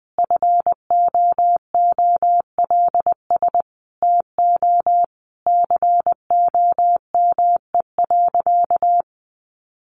Morse code, heard as a keyed tone, about 19 wpm, 700 Hz. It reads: FOOLS TO COME.
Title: A Sound Of Thunder 20wpm